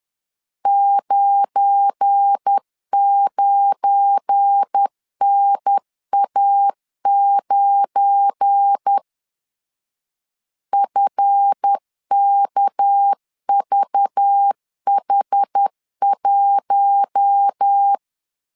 Morse code, heard as a keyed tone, about 10 wpm, 800 Hz. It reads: 99NA9 FKVH1
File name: LEZIONI / 7 GF9Z / 7 GF9 10wpm 2747